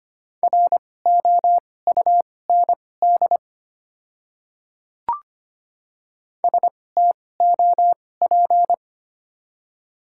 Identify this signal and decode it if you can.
Morse code, heard as a keyed tone, about 25 wpm, spaced as for 18 wpm, 700 Hz. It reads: ROUND STOP
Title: 2x Character Spacing - Top 300 Words - 25wpm